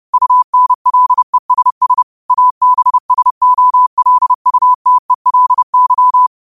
Morse code, heard as a keyed tone, about 30 wpm, 1000 Hz. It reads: UNLESS ABSOLUTELY